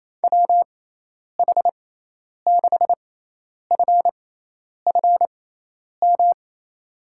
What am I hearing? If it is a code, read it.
Morse code, heard as a keyed tone, about 28 wpm, spaced as for 10 wpm, 700 Hz. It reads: WH6FFM
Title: USA Call Signs - 6x Character Spacing - 28wpm